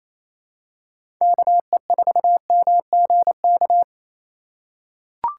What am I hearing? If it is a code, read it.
Morse code, heard as a keyed tone, about 28 wpm, 700 Hz. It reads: KE4MGK
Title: USA Call Signs - 6 Characters - 4x Character Spacing - 28wpm